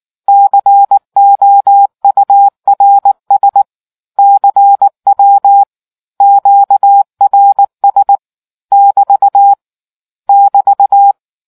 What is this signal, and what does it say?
Morse code, heard as a keyed tone, about 19 wpm, 800 Hz. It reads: COURS CW QRS <BT> <BT>